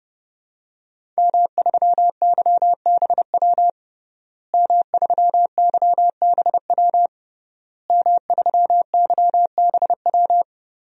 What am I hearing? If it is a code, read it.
Morse code, heard as a keyed tone, about 29 wpm, 700 Hz. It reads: M3YBW M3YBW M3YBW
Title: DXCC-100 Call Signs - 5 Characters - Repeated 3x - 30wpm